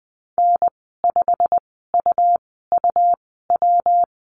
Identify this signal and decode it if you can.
Morse code, heard as a keyed tone, about 20 wpm, spaced as for 14 wpm, 700 Hz. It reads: N5UUW
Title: 140 Lesson 23g - Letter f - Call Signs 20wpm